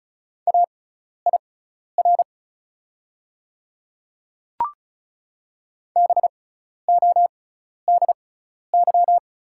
Morse code, heard as a keyed tone, about 36 wpm, spaced as for 12 wpm, 700 Hz. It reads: AIR BODY
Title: 6x Character Spacing - Top 500 Words - 35wpm